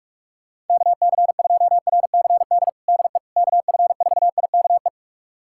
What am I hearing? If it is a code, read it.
Morse code, heard as a keyed tone, about 46 wpm, 700 Hz. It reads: KC2RCD DE KF4IKE
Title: USA Callsign DE Callsign 45wpm